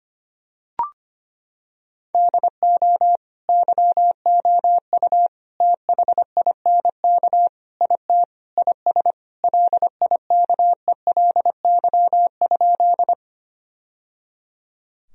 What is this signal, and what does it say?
Morse code, heard as a keyed tone, about 25 wpm, 700 Hz. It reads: DO YOU THINK IT IS LIKELY?
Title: Sentences from Top 1000 Words - Review Part 4 - 25wpm